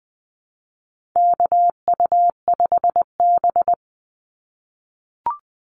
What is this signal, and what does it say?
Morse code, heard as a keyed tone, about 20 wpm, 700 Hz. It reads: KU5B
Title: Call Signs - 20wpm